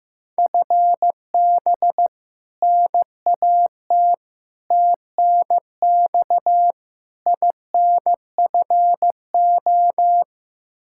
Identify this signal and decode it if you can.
Morse code, heard as a keyed tone, about 15 wpm, 700 Hz. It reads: FB NAT TNX INFO
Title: Practice QSO - Exchange 3 - More Info - 15wpm